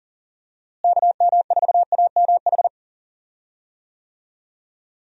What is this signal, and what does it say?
Morse code, heard as a keyed tone, about 40 wpm, 700 Hz. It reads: KM4AMH
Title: USA Call Signs - 6 Characters - Repeated 3x - 40wpm